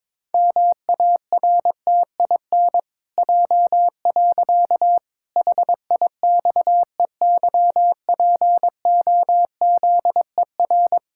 Morse code, heard as a keyed tone, about 22 wpm, 700 Hz. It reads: MARTIN J. HIXEYPOZER
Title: Pangram Challenge v2 - 22wpm